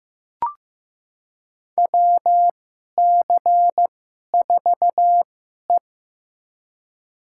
Morse code, heard as a keyed tone, about 16 wpm, spaced as for 11 wpm, 700 Hz. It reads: WC4E
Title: USA Call Signs - 4 Characters - 2x Character Spacing - 15wpm